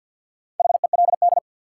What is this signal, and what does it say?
Morse code, heard as a keyed tone, about 52 wpm, 700 Hz. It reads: HELD
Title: Sentences from Top 1600 Words - New Word Drill - 50wpm